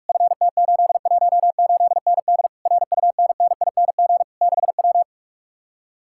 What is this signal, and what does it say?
Morse code, heard as a keyed tone, about 49 wpm, 700 Hz. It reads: FT818ND RUNNING 6W